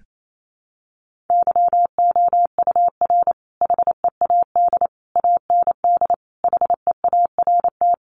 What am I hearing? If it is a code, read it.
Morse code, heard as a keyed tone, about 28 wpm, 700 Hz. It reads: YOUR HEAD AND HEART
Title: Sentences from Top 1300 Words 28wpm